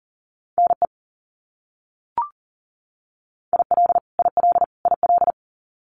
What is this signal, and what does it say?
Morse code, heard as a keyed tone, about 40 wpm, 700 Hz. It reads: NE IL IL IL
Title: US States Repeated 3x 40wpm